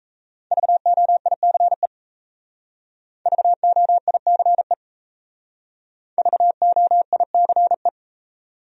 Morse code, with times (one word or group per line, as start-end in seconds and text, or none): 0.5-1.9 VOICE
3.3-4.7 VOICE
6.2-7.9 VOICE